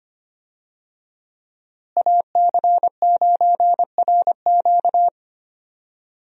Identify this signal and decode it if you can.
Morse code, heard as a keyed tone, about 25 wpm, 700 Hz. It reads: AC9RQ
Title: USA Call Signs - 5 Characters - Repeated 2x - 25wpm